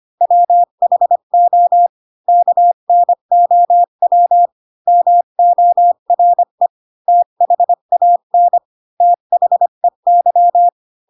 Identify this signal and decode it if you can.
Morse code, heard as a keyed tone, about 25 wpm, 700 Hz. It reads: WHO KNOW MORE THAN THEY
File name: Pinocchio / Pinocchio ch4 20 25 ik1hge